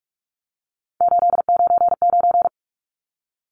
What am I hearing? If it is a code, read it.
Morse code, heard as a keyed tone, about 44 wpm, 700 Hz. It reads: 899
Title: Numbers - 3-Digit 45wpm